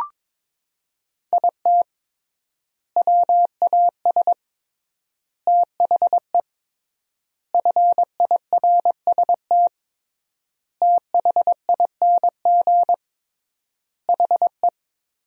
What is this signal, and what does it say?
Morse code, heard as a keyed tone, about 22 wpm, 700 Hz. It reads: IT WAS THE FIRST THING HE
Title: 3x Word-Spacing - Sentences from Top 300 Words - 22wpm